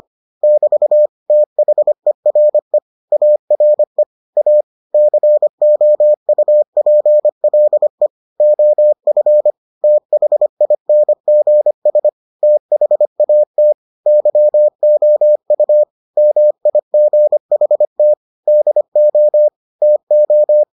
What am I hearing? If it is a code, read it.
Morse code, heard as a keyed tone, about 25 wpm, 600 Hz. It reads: <BT> THERE ARE A COUPLE OF THINGS THAT YOU MIGHT DO TO